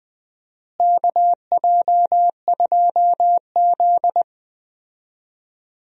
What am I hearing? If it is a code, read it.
Morse code, heard as a keyed tone, about 20 wpm, 700 Hz. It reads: KJ2Z
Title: USA Call Signs - 4 Characters - Repeated 2x - 20wpm